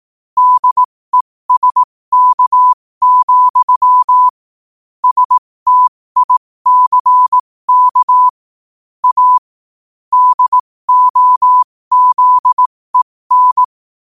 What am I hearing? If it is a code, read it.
Morse code, heard as a keyed tone, about 18 wpm, spaced as for 15 wpm, 1000 Hz. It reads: DESK, STICK A DOZEN
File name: qotdfiles / 15wpm